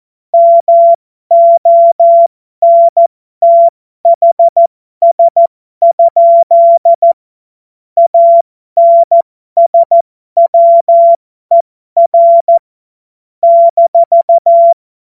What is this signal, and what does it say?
Morse code, heard as a keyed tone, about 14 wpm, spaced as for 12 wpm, 680 Hz. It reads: MONTHS? ANSWER -
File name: Morsekurs / Witze